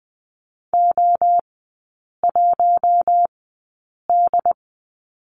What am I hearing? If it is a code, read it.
Morse code, heard as a keyed tone, about 20 wpm, spaced as for 8 wpm, 700 Hz. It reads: O1D
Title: 067 Lesson 12e - Letter d - Three Characters 20wpm